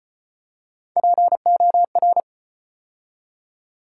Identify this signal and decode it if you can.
Morse code, heard as a keyed tone, about 33 wpm, 700 Hz. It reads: POR